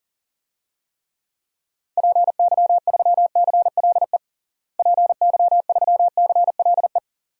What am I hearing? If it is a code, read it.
Morse code, heard as a keyed tone, about 40 wpm, 700 Hz. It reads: PY3CLE PY3CLE
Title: DXCC-100 Call Signs - 6 Characters - Repeated 2x - 40wpm